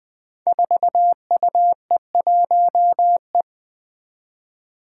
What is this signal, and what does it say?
Morse code, heard as a keyed tone, about 20 wpm, 700 Hz. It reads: 4UE1E
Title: Five Challenging Letter-Number Combinations 20wpm